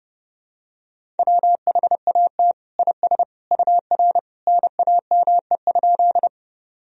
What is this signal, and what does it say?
Morse code, heard as a keyed tone, about 30 wpm, 700 Hz. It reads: WHAT IS UR NAME?
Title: Sentences from Top 200 Words and Abbreviations 30wpm